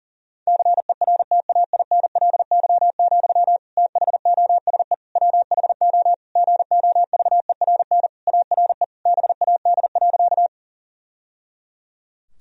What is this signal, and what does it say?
Morse code, heard as a keyed tone, about 40 wpm, 700 Hz. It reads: CERTAINLY, THOSE WHO GOVERN ARE BAD.
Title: Sentences from Top 1000 Words - Review Part 2 - 40wpm